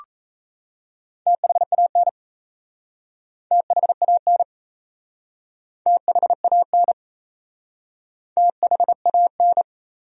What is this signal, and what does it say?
Morse code, heard as a keyed tone, about 37 wpm, 700 Hz. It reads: THAN THAN THAN THAN